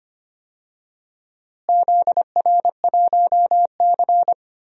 Morse code, heard as a keyed tone, about 25 wpm, 700 Hz. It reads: ZR1C
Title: DXCC-100 Call Signs - 4 Characters - Repeated 3x - 25wpm